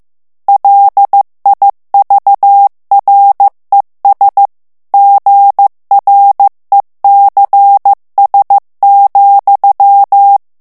Morse code, heard as a keyed tone, about 15 wpm, 800 Hz. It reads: LIVRES GRECS,
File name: FRANCAIS TEXTE CLAIR 2 VITESSE 14 MOTS MINUTES